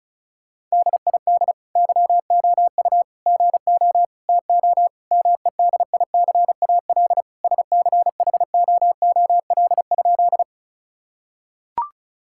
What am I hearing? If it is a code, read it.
Morse code, heard as a keyed tone, about 36 wpm, 700 Hz. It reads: DID YOU GO TO MEDICAL SCHOOL?
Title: Sentences from Top 1000 Words - Review Part 5 - 35wpm